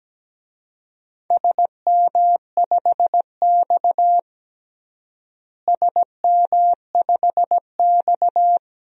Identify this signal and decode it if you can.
Morse code, heard as a keyed tone, about 17 wpm, 700 Hz. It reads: SM5X SM5X
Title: DXCC-100 Call Signs - 4 Characters - Repeated 2x - 17wpm